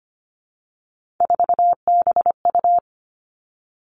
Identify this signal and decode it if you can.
Morse code, heard as a keyed tone, about 25 wpm, 700 Hz. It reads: 4BU
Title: Three Challenging Letter-Number Combinations 25wpm